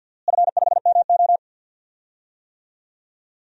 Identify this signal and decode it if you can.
Morse code, heard as a keyed tone, about 50 wpm, 700 Hz. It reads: F5MO
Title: DXCC-100 Call Signs - 4 Characters - Repeated 2x - 50wpm